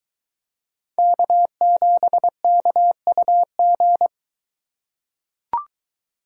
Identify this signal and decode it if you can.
Morse code, heard as a keyed tone, about 21 wpm, 700 Hz. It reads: K7KUG